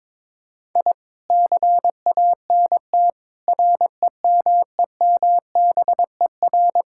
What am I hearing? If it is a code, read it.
Morse code, heard as a keyed tone, about 21 wpm, 700 Hz. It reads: I CANT REMEMBER
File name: practice / 22wpm / Sets of 7 Words - 22wpm